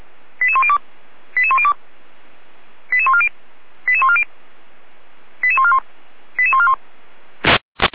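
Five-tone selective call (ZVEI), heard as quick runs of five tones, tones of about 70 ms. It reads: 88102 88102 88140 88140 88141 88141
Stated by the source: ZVEI five-tone selective call received with an SDR.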